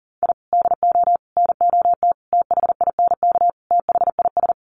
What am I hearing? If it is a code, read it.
Morse code, heard as a keyed tone, about 37 wpm, 700 Hz. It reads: I DO NOT THINK THIS
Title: Sets of 5 Words Encore 40wpm